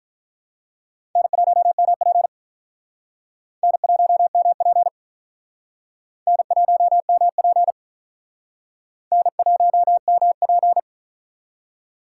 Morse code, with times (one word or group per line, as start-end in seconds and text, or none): 1.2-2.3 N1MP
3.6-4.9 N1MP
6.3-7.7 N1MP
9.1-10.8 N1MP